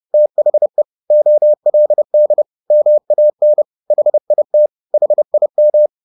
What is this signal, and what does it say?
Morse code, heard as a keyed tone, about 30 wpm, 600 Hz. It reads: THE OLD MAN HIT HIM